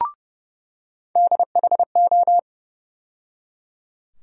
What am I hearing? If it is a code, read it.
Morse code, heard as a keyed tone, about 30 wpm, 700 Hz. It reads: DHO
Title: USA Call Sign Postfixes - 3 Letters 30wpm